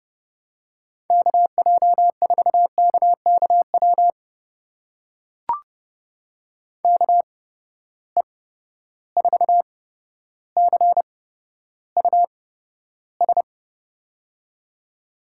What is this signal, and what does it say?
Morse code, heard as a keyed tone, about 30 wpm, spaced as for 13 wpm, 700 Hz. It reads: KJ4KKW KE4CUS
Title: USA Call Signs - 6 Characters - 8x Character Spacing - 30wpm